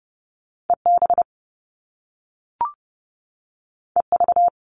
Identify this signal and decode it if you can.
Morse code, heard as a keyed tone, about 30 wpm, 700 Hz. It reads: EB EV
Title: Top Two Letter Combinations in Words 30wpm